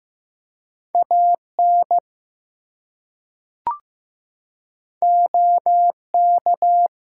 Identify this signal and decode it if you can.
Morse code, heard as a keyed tone, about 15 wpm, 700 Hz. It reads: AN OK
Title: Two Letter Words - 15wpm